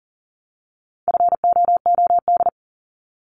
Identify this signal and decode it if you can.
Morse code, heard as a keyed tone, about 39 wpm, 700 Hz. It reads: FOOD
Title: Multi-letter Phonemes 40wpm